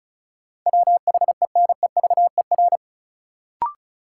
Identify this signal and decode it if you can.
Morse code, heard as a keyed tone, about 34 wpm, 700 Hz. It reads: WHENEVER
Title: Sentences from Top 1200 Words - New Word Drill - 35wpm